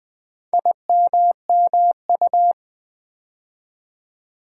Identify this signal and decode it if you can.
Morse code, heard as a keyed tone, about 20 wpm, 700 Hz. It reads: IMMU
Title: Build A Word - 2 Letters At A Time - 6-Letter Words - 20wpm